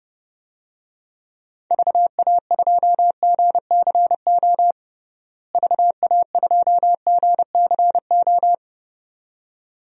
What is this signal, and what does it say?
Morse code, heard as a keyed tone, about 30 wpm, 700 Hz. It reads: VA2GCO VA2GCO
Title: DXCC-100 Call Signs - 6 Characters - Repeated 2x - 30wpm